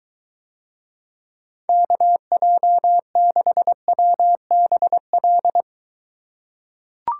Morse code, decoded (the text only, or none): KJ6WBL